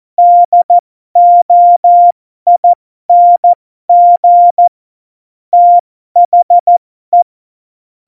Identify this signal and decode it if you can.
Morse code, heard as a keyed tone, about 14 wpm, spaced as for 12 wpm, 700 Hz. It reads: DOING THE
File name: Morsekurs / Witze